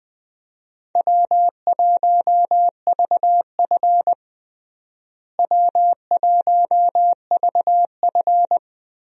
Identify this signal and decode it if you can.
Morse code, heard as a keyed tone, about 21 wpm, 700 Hz. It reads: W1VF W1VF
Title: USA Call Signs - 4 Characters - Repeated 3x - 20wpm